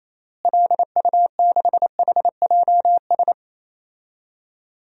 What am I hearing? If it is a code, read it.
Morse code, heard as a keyed tone, about 28 wpm, 700 Hz. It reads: LU6HJS
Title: DXCC-100 Call Signs - 6 Characters - Repeated 3x - 28wpm